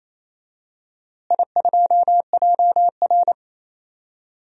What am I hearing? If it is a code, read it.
Morse code, heard as a keyed tone, about 28 wpm, 700 Hz. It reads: I2JR
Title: DXCC-100 Call Signs - 4 Characters - Repeated 2x - 28wpm